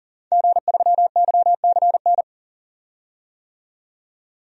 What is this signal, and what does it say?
Morse code, heard as a keyed tone, about 40 wpm, 700 Hz. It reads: G3YCN
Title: DXCC-100 Call Signs - 5 Characters - 40wpm